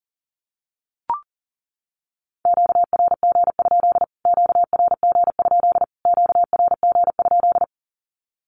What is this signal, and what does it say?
Morse code, heard as a keyed tone, about 40 wpm, 700 Hz. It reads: QRG? QRG? QRG?
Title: Q-Codes Repeated 3x 40wpm